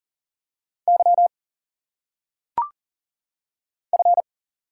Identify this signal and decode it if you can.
Morse code, heard as a keyed tone, about 40 wpm, 700 Hz. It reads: Y F
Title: Single Letters - Phonetic Pronunciation 40wpm